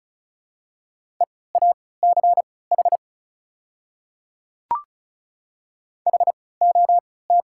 Morse code, heard as a keyed tone, about 34 wpm, spaced as for 20 wpm, 690 Hz. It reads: EACH HOT
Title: 3x Character Spacing - Top 100 Words - 35wpm